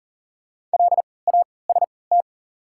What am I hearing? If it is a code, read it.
Morse code, heard as a keyed tone, about 38 wpm, spaced as for 22 wpm, 700 Hz. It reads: LAST
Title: 3x Character Spacing - Top 300 Words - 40wpm